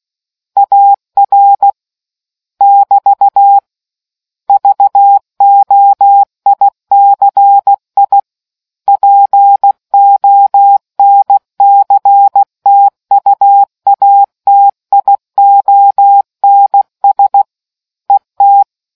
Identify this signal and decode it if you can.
Morse code, heard as a keyed tone, about 16 wpm, 800 Hz. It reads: AR <BT> VOICI PONCTUATIONS ET